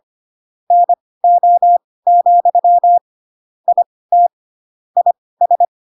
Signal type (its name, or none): Morse code